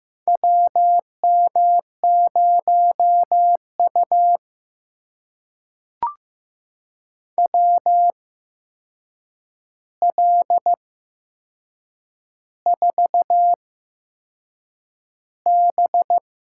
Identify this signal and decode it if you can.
Morse code, heard as a keyed tone, about 15 wpm, spaced as for 6 wpm, 700 Hz. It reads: WM0U WL4B
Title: USA Call Signs - 4 Characters - 8x Character Spacing - 15wpm